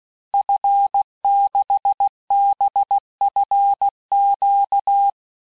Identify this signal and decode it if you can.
Morse code, heard as a keyed tone, about 16 wpm, 800 Hz. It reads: F6BFQ
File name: cours-du-lundi files / cours-1580-fin-QSO